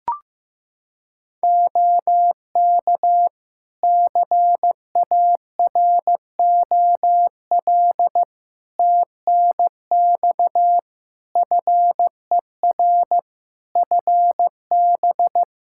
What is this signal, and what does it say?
Morse code, heard as a keyed tone, about 15 wpm, 700 Hz. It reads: OK CAROL TNX FER FB
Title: K1USN SST QSO Practice 15wpm